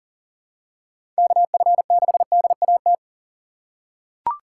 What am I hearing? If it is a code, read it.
Morse code, heard as a keyed tone, about 40 wpm, 700 Hz. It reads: KF6DAT
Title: USA Call Signs - 6 Characters - 2x Character Spacing - 40wpm